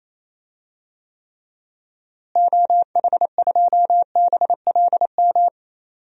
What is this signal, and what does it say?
Morse code, heard as a keyed tone, about 27 wpm, 700 Hz. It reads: OH2BLM
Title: DXCC-100 Call Signs - 6 Characters - Repeated 2x - 28wpm